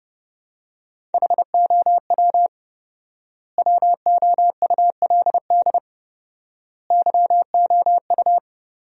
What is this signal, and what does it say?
Morse code, heard as a keyed tone, about 30 wpm, 700 Hz. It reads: HOW WOULD YOU
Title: 4x Word-Spacing - Sentences from Top 100 Words - 30wpm